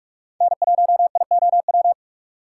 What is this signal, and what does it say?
Morse code, heard as a keyed tone, about 41 wpm, 700 Hz. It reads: N1IOW
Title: USA Call Signs - 5 Characters - 2x Character Spacing - 45wpm